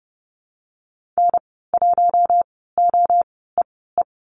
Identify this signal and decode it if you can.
Morse code, heard as a keyed tone, about 30 wpm, spaced as for 17 wpm, 700 Hz. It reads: N1OEE
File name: Lesson 19g - Letter w - Call Signs 30wpm